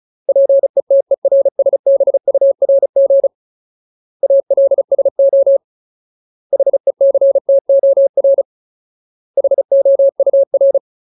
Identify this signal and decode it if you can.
Morse code, heard as a keyed tone, about 34 wpm, 550 Hz. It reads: PETERSBURG ALSO HECTOR HOUR
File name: Ultimate Word List / 35WPM Ultimate